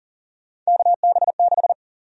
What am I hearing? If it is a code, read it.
Morse code, heard as a keyed tone, about 40 wpm, 700 Hz. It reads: KB6
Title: USA Call Sign Prefixes - Repeated 2x - 40wpm